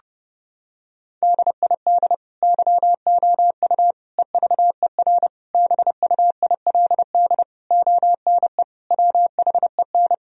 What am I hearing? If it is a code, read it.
Morse code, heard as a keyed tone, about 30 wpm, 700 Hz. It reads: DID YOU EVER BUILD ONE WHEN
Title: Sentences from Top 1000 Words - Review Part 1 - 30wpm